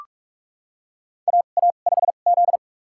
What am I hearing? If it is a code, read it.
Morse code, heard as a keyed tone, about 45 wpm, spaced as for 32 wpm, 700 Hz. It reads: AA5Z
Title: USA Call Signs - 4 Characters - 2x Character Spacing - 45wpm